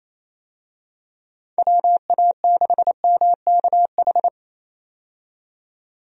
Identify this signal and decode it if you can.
Morse code, heard as a keyed tone, about 28 wpm, 700 Hz. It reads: WA6MKH
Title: USA Call Signs - 6 Characters - 28wpm